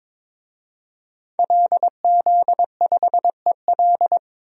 Morse code, heard as a keyed tone, about 22 wpm, 700 Hz. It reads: LZ5EL
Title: DXCC-100 Call Signs - 5 Characters - 22wpm